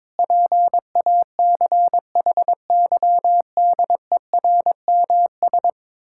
Morse code, heard as a keyed tone, about 21 wpm, 700 Hz. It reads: PACHYDERMS
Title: Pangram Challenge v2 - 22wpm